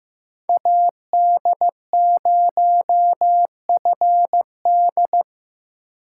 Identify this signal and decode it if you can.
Morse code, heard as a keyed tone, about 15 wpm, 700 Hz. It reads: AD0FD
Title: USA Call Signs - 4x Character Spacing - 15wpm